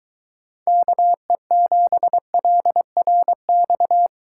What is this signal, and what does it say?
Morse code, heard as a keyed tone, about 23 wpm, 700 Hz. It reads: KE7LRX